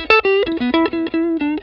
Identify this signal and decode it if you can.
There is no code.